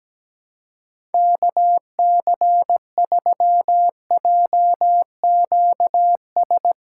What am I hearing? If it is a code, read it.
Morse code, heard as a keyed tone, about 17 wpm, 700 Hz. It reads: KC3JQS
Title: USA Callsign DE Callsign 17wpm